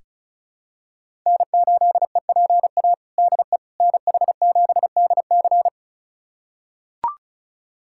Morse code, heard as a keyed tone, about 36 wpm, 700 Hz. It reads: N8EPA DE NH7DC
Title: USA Callsign DE Callsign - 5 Characters - 35wpm